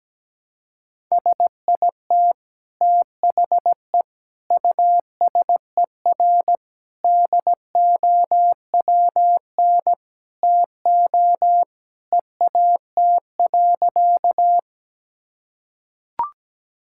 Morse code, heard as a keyed tone, about 17 wpm, 700 Hz. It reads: SIT THE USER DOWN TO EAT.
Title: Sentences from Top 1000 Words - Review Part 5 - 17wpm